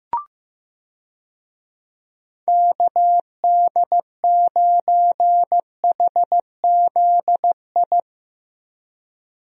Morse code, heard as a keyed tone, about 15 wpm, 700 Hz. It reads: KD9HZI
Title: USA Call Signs - 6 Characters - Repeated 2x - 15wpm